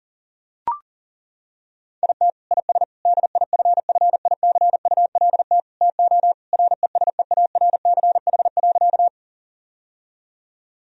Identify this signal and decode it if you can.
Morse code, heard as a keyed tone, about 40 wpm, 700 Hz. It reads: IT IS DIFFICULT TO RESEARCH.
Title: Sentences from Top 1000 Words - Review Part 4 - 40wpm